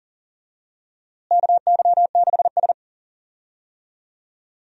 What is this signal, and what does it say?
Morse code, heard as a keyed tone, about 40 wpm, 700 Hz. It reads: KY6S
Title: USA Call Signs - 4 Characters - Repeated 2x - 40wpm